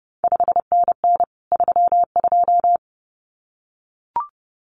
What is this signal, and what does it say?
Morse code, heard as a keyed tone, about 30 wpm, 700 Hz. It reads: 5NN 32
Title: CQ WW DX Contest Exchange - 30wpm